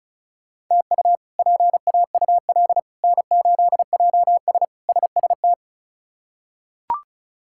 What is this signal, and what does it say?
Morse code, heard as a keyed tone, about 34 wpm, 700 Hz. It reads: TU PAUL N8JS SST
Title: K1USN SST QSO Practice - 35wpm